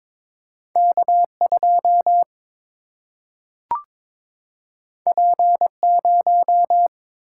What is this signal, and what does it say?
Morse code, heard as a keyed tone, about 22 wpm, 700 Hz. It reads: K 2 P 0
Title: Single-Letter Single-Number 22wpm